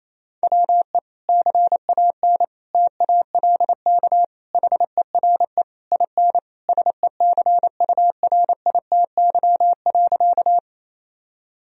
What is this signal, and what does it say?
Morse code, heard as a keyed tone, about 28 wpm, 700 Hz. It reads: WE CAN TALK HERE IN SECURITY.
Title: Sentences from Top 1000 Words - Review Part 4 - 28wpm